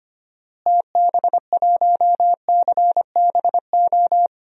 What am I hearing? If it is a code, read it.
Morse code, heard as a keyed tone, about 25 wpm, 700 Hz. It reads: TB1CBO
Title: DXCC-100 Call Signs - 6 Characters - Repeated 3x - 25wpm